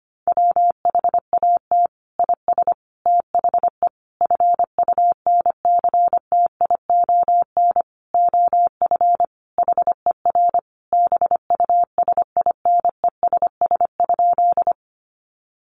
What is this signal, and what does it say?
Morse code, heard as a keyed tone, about 25 wpm, 700 Hz. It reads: WHAT IS THE FUNCTION OF HER BUSINESS?
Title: Sentences from Top 1100 Words 25wpm